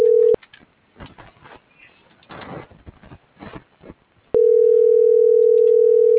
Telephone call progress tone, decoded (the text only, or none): ringback tone